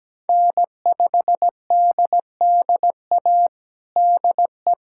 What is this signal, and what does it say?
Morse code, heard as a keyed tone, about 17 wpm, 700 Hz. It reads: N5DDA DE
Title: USA Callsign DE Callsign - 5 Characters - 17wpm